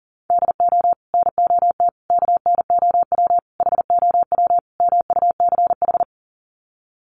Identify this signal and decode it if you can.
Morse code, heard as a keyed tone, about 40 wpm, 700 Hz. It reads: DO NOT KNOW HOW MUCH
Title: Sets of 5 Words Encore 40wpm